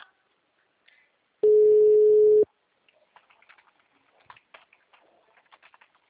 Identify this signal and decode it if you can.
Ringback tone (Europe, 425 Hz).
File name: out- -2017